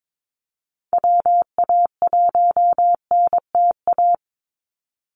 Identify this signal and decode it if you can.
Morse code, heard as a keyed tone, about 22 wpm, 700 Hz. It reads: WA1NTA
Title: Call Signs Repeated 2x - 22wpm